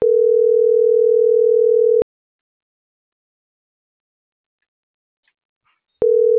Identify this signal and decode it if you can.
Ringback tone (North America), 440 + 480 Hz.